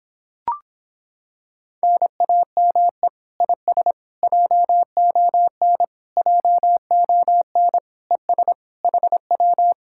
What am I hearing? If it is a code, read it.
Morse code, heard as a keyed tone, about 26 wpm, 700 Hz. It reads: NAME IS JON JON ES HW